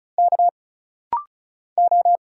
Morse code, heard as a keyed tone, about 36 wpm, 700 Hz. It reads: K O
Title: Single Letters - Rapid-Fire 35wpm